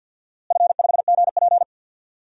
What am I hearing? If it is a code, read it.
Morse code, heard as a keyed tone, about 51 wpm, 700 Hz. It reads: F5GP
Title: DXCC-100 Call Signs - 4 Characters - Repeated 2x - 50wpm